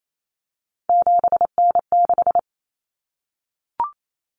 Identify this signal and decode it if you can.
Morse code, heard as a keyed tone, about 28 wpm, 700 Hz. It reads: 7N6
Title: Common Cut Numbers - 3-Digit 28wpm